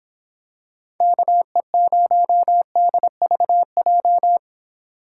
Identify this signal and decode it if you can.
Morse code, heard as a keyed tone, about 25 wpm, 700 Hz. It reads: KE0DVJ